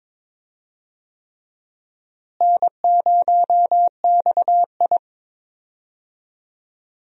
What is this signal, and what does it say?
Morse code, heard as a keyed tone, about 22 wpm, 700 Hz. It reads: N0XI
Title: USA Call Signs - 4 Characters - Repeated 3x - 22wpm